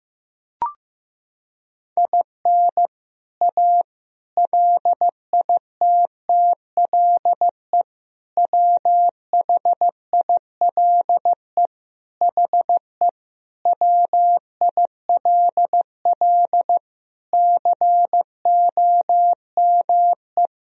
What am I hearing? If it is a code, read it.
Morse code, heard as a keyed tone, about 16 wpm, 700 Hz. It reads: IN A LITTLE WHILE HE WILL COME
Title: Sentences from Top 1000 Words - Review Part 1 - 15wpm